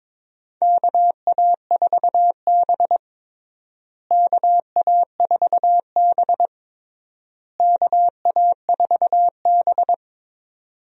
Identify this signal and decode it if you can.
Morse code, heard as a keyed tone, about 22 wpm, 700 Hz. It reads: KA4B KA4B KA4B
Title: USA Call Signs - 4 Characters - Repeated 3x - 22wpm